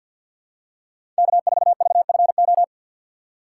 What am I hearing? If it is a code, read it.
Morse code, heard as a keyed tone, about 53 wpm, 700 Hz. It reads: K4VFO
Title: USA Call Signs - 5 Characters - 6x Character Spacing - 50wpm